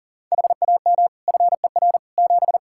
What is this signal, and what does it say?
Morse code, heard as a keyed tone, about 40 wpm, 700 Hz. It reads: HAM FER 7
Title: Practice QSO - Exchange 3 - More Info - 40wpm